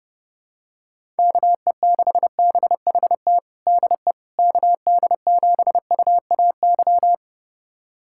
Morse code, heard as a keyed tone, about 32 wpm, 700 Hz. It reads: KE6BHT DE KD7UAY
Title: USA Callsign DE Callsign 30wpm